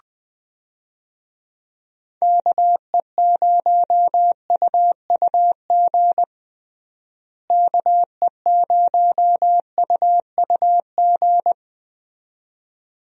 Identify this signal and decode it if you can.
Morse code, heard as a keyed tone, about 20 wpm, 700 Hz. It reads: KE0UUG KE0UUG
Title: USA Call Signs - 6 Characters - Repeated 2x - 20wpm